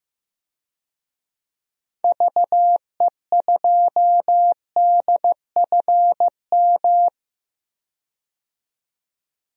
Morse code, heard as a keyed tone, about 15 wpm, 700 Hz. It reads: VE2DFM
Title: DXCC-100 Call Signs - 6 Characters - 15wpm